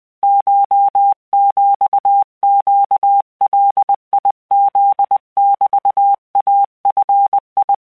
Morse code (Japanese Wabun code), heard as a keyed tone, about 20 wpm, 800 Hz. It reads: コヒネガフ メイヂ